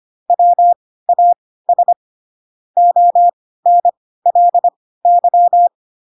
Morse code, heard as a keyed tone, about 25 wpm, spaced as for 16 wpm, 700 Hz. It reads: WAS ONLY